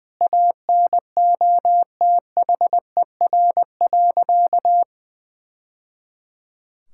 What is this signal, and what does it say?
Morse code, heard as a keyed tone, about 21 wpm, 700 Hz. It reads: ANOTHER.
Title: Sentences from Top 1000 Words - Review Part 6 - 20wpm